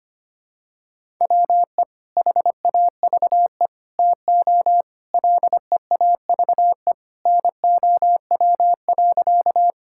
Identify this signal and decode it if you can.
Morse code, heard as a keyed tone, about 25 wpm, 700 Hz. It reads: WE HAVE TO LEAVE NOW.
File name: practice / 25wpm / Sentences with 3-Word Phrases - 25wpm